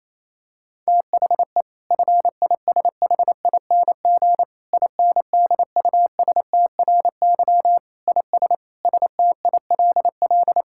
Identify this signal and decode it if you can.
Morse code, heard as a keyed tone, about 28 wpm, 700 Hz. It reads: THE FISHING INDUSTRY IS STILL
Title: Sentences from Top 1000 Words - Review Part 6 - 28wpm